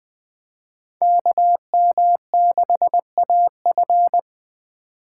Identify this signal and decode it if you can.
Morse code, heard as a keyed tone, about 20 wpm, 700 Hz. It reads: KM6AF
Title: USA Call Signs - 2x Character Spacing - 20wpm